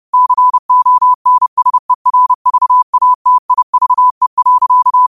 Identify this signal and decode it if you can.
Morse code, heard as a keyed tone, about 30 wpm, 1000 Hz. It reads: CONSERVATIVE.
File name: qotdfiles / 30wpm